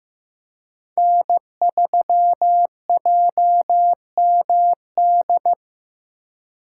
Morse code, heard as a keyed tone, about 15 wpm, 700 Hz. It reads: N3JMD
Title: USA Call Signs - 4x Character Spacing - 15wpm